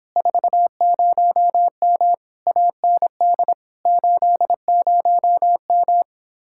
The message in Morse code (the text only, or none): 40M AND 80M